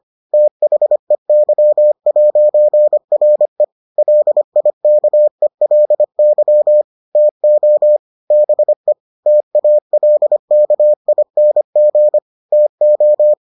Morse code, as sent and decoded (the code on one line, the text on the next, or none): - .... -.-- .----. .-. / .-.. .. -.- . .-.. -.-- / - --- / -... / - .- .-.. -.- .. -. --. / - ---
THEY'RE LIKELY TO BE TALKING TO